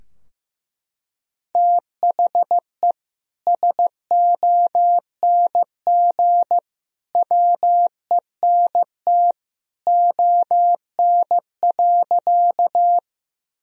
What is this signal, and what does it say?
Morse code, heard as a keyed tone, about 15 wpm, 700 Hz. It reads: THE SONG WENT ON.
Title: Sentences from Top 1000 Words - Review Part 2 - 15wpm